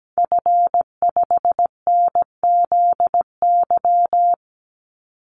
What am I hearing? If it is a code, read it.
Morse code, heard as a keyed tone, about 17 wpm, 700 Hz. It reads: F5NZY
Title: Call Signs Repeated 2x - 17wpm